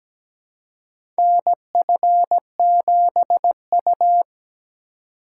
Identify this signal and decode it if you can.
Morse code, heard as a keyed tone, about 17 wpm, 700 Hz. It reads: NF7U
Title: USA Call Signs - 3x Character Spacing - 17wpm